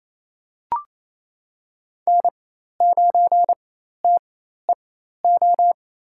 Morse code, heard as a keyed tone, about 28 wpm, spaced as for 13 wpm, 700 Hz. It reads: N9TEO
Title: USA Call Signs - 4x Character Spacing - 28wpm